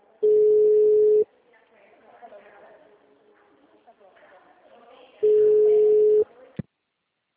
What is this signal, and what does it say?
Ringback tone (Europe, 425 Hz), with speech.